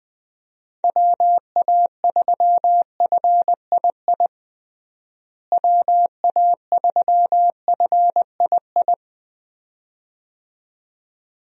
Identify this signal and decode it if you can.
Morse code, heard as a keyed tone, about 21 wpm, 700 Hz. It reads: WA3FII WA3FII
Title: USA Call Signs - 6 Characters - Repeated 2x - 20wpm